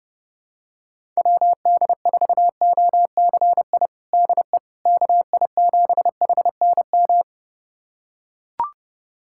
Morse code, heard as a keyed tone, about 32 wpm, 700 Hz. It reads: WD4OCI DE KI7HNM
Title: USA Callsign DE Callsign - 6 Characters - 30wpm